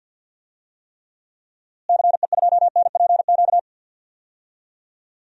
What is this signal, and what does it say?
Morse code, heard as a keyed tone, about 50 wpm, 700 Hz. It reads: CE2NPQ
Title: DXCC-100 Call Signs - 6 Characters - Repeated 3x - 50wpm